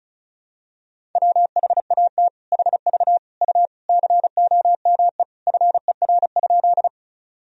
Morse code, heard as a keyed tone, about 34 wpm, 700 Hz. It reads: WHAT HV U COME FER?
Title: Sentences from Top 100 Words and Abbreviations 35wpm